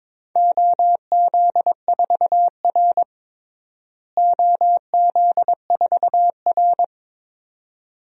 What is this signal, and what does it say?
Morse code, heard as a keyed tone, about 22 wpm, 700 Hz. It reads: OZ4R OZ4R
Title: DXCC-100 Call Signs - 4 Characters - Repeated 2x - 22wpm